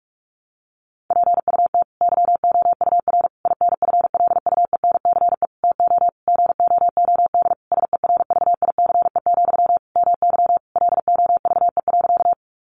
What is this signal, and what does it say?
Morse code, heard as a keyed tone, about 46 wpm, 700 Hz. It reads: PUT YOUR INFLUENCE TO GOOD SERVICE, MY LOVE.
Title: Sentences from Top 1100 Words 45wpm